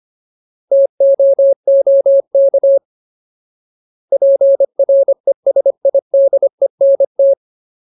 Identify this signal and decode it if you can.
Morse code, heard as a keyed tone, about 25 wpm, 550 Hz. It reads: TOOK PRESIDENT
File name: Ultimate Word List / 25WPM Ultimate